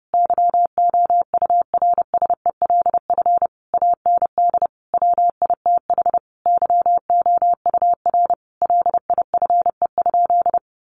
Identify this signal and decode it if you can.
Morse code, heard as a keyed tone, about 30 wpm, 700 Hz. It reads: YOURSELF AND WITH YOUR LIFE?
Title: Sentences from Top 900 Words 30wpm